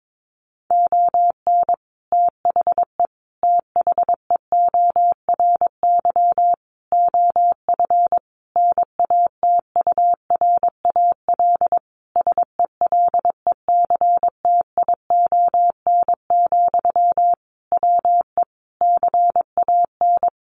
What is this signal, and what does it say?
Morse code, heard as a keyed tone, about 22 wpm, 700 Hz. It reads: ON THE THEORY OF NATURAL SELECTION, WE CAN
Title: Sentences from Top 1400 Words 22wpm